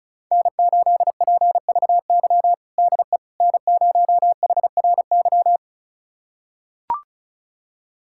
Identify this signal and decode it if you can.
Morse code, heard as a keyed tone, about 36 wpm, 700 Hz. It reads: N8PVY DE N0HRY
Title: USA Callsign DE Callsign - 5 Characters - 35wpm